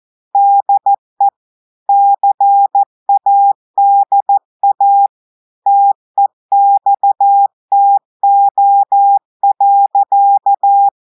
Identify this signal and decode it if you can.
Morse code, heard as a keyed tone, about 14 wpm, 800 Hz. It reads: DE CADA TEXTO.
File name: audios / portugais / portugais texte en clair 1 vitesse 14 mots minutes